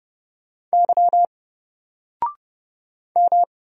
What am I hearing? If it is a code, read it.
Morse code, heard as a keyed tone, about 29 wpm, 700 Hz. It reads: Y M
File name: Single-Letter-Number-30wpm